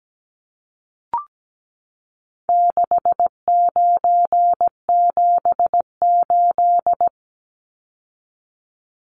Morse code, heard as a keyed tone, about 17 wpm, 700 Hz. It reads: 6978
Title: Numbers - 4-Digit 17wpm